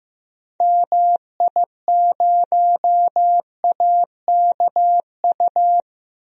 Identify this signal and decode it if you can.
Morse code, heard as a keyed tone, about 15 wpm, 700 Hz. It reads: MI0AKU
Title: DXCC-100 Call Signs - 6 Characters - 15wpm